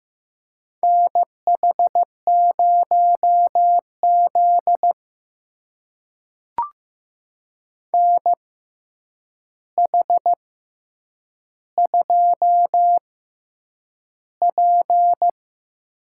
Morse code, heard as a keyed tone, about 15 wpm, spaced as for 8 wpm, 700 Hz. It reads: NH0Z NH2P
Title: USA Call Signs - 4 Characters - 6x Character Spacing - 15wpm